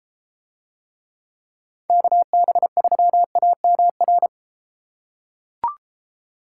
Morse code, heard as a keyed tone, about 31 wpm, 700 Hz. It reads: KB3AMR